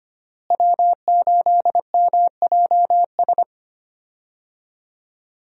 Morse code, heard as a keyed tone, about 25 wpm, 700 Hz. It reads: W8MJS
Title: USA Call Signs - 5 Characters - Repeated 2x - 25wpm